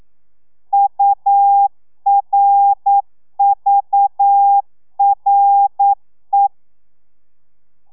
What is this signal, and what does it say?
Morse code, heard as a keyed tone, about 9 wpm, 800 Hz. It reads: URVRE
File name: telegrafia / i2tfj / Lez6 40